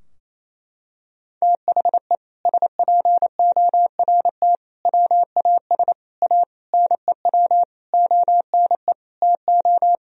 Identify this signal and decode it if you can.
Morse code, heard as a keyed tone, about 28 wpm, 700 Hz. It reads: THE SPORT WAS A NEW ONE TO
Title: Sentences from Top 1000 Words - Review Part 4 - 28wpm